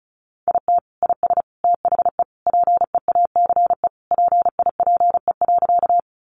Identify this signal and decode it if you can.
Morse code, heard as a keyed tone, about 34 wpm, 700 Hz. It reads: IT IS THE PEACE PIPE.
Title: Sentences from Top 1500 Words 35wpm